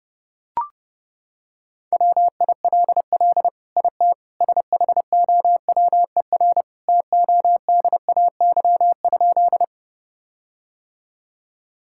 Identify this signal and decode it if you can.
Morse code, heard as a keyed tone, about 32 wpm, 700 Hz. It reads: WILL IT SHOWER TODAY?
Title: Sentences from Top 1600 Words 30wpm